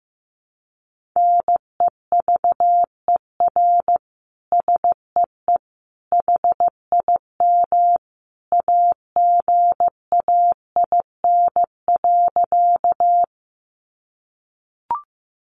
Morse code, heard as a keyed tone, about 15 wpm, 700 Hz. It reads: NEVER SEE HIM AGAIN.
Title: Sentences from Top 300 Words 15wpm